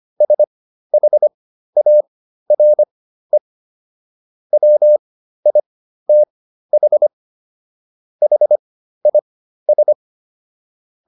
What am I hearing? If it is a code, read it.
Morse code, heard as a keyed tone, about 25 wpm, spaced as for 13 wpm, 600 Hz. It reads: SHARE WITH HIS